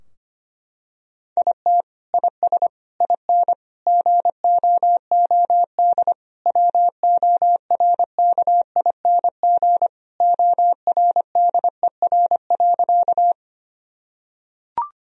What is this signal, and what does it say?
Morse code, heard as a keyed tone, about 25 wpm, 700 Hz. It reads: IT IS IN GOOD WORKING ORDER.
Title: Sentences from Top 1000 Words - Review Part 4 - 25wpm